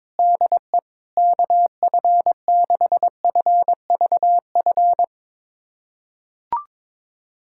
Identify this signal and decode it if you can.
Morse code, heard as a keyed tone, about 22 wpm, 700 Hz. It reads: DE KF6FVF
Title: USA Callsign DE Callsign - 6 Characters - 22wpm